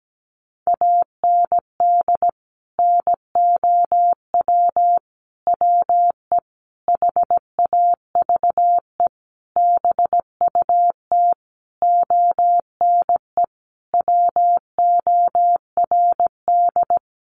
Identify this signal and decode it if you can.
Morse code, heard as a keyed tone, about 17 wpm, 700 Hz. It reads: AND NOW WE HAVE BUT ONE WORD
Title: Sentences from Top 1500 Words 17wpm